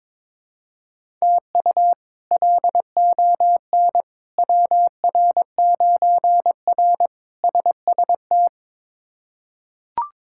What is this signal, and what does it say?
Morse code, heard as a keyed tone, about 22 wpm, 700 Hz. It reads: TU LON WR9R SST
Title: K1USN SST QSO Practice - 22wpm